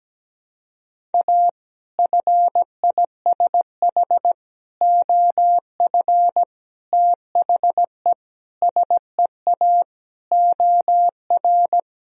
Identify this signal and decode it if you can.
Morse code, heard as a keyed tone, about 17 wpm, 700 Hz. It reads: A FISH OF THE SEA OR
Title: Sentences from Top 1000 Words - Review Part 1 - 17wpm